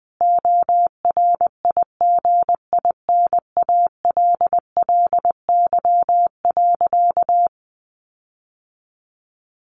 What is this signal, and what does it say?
Morse code, heard as a keyed tone, about 20 wpm, 700 Hz. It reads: ORIGINALLY.
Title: Sentences from Top 1500 Words 20wpm